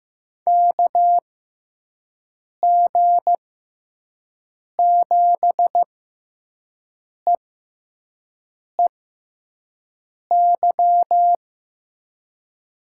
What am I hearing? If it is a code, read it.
Morse code, heard as a keyed tone, about 15 wpm, spaced as for 5 wpm, 700 Hz. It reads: KG7EEY
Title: USA Call Signs - 6 Characters - 6x Character Spacing - 15wpm